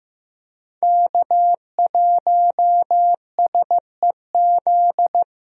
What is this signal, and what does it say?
Morse code, heard as a keyed tone, about 15 wpm, 700 Hz. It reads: K1SEZ
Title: USA Call Signs - 5 Characters - 15wpm